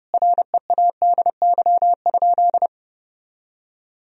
Morse code, heard as a keyed tone, about 30 wpm, 700 Hz. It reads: READY?
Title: Sentences from Top 1000 Words - Review Part 1 - 30wpm